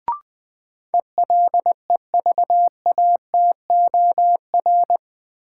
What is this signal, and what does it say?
Morse code, heard as a keyed tone, about 20 wpm, 700 Hz. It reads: ELEVATOR
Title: Sentences from Top 1600 Words - New Word Drill - 20wpm